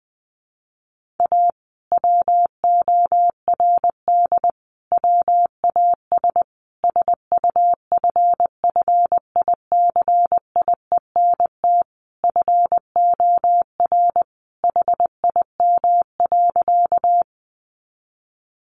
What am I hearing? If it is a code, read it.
Morse code, heard as a keyed tone, about 20 wpm, 700 Hz. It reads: A WORD WAS SUFFICIENT FOR HIM.
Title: Sentences from Top 1400 Words 20wpm